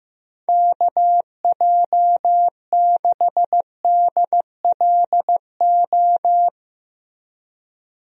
Morse code, heard as a keyed tone, about 14 wpm, 700 Hz. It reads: KJ6DLO
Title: USA Call Signs - 6 Characters - Repeated 3x - 15wpm